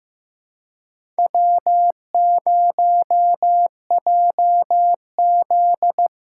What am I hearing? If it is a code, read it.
Morse code, heard as a keyed tone, about 15 wpm, 700 Hz. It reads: W0JZ
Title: USA Call Signs - 4 Characters - 2x Character Spacing - 15wpm